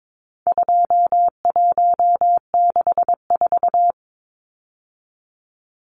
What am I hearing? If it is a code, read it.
Morse code, heard as a keyed tone, about 22 wpm, 700 Hz. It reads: 2164
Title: Numbers - 4-Digit 22wpm